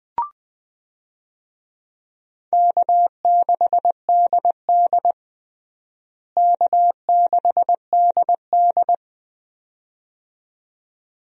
Morse code, heard as a keyed tone, about 20 wpm, 700 Hz. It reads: K6DD K6DD
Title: USA Call Signs - 4 Characters - Repeated 2x - 20wpm